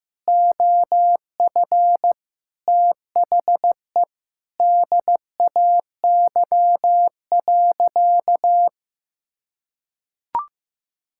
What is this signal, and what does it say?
Morse code, heard as a keyed tone, about 15 wpm, 700 Hz. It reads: OF THE DAY.
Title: Sentences from Top 1000 Words - Review Part 6 - 15wpm